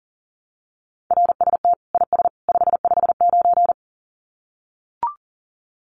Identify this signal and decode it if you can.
Morse code, heard as a keyed tone, about 40 wpm, 700 Hz. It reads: RST IS 559
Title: Common QSO Phrases 40wpm